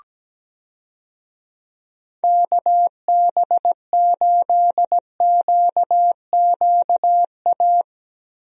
Morse code, heard as a keyed tone, about 17 wpm, 700 Hz. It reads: KB8QQA
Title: USA Call Signs - 6 Characters - Repeated 2x - 17wpm